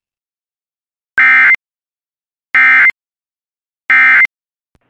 Emergency Alert System (SAME) tones, heard as harsh buzzing bursts. It is heard from 1.2 to 4.3 s.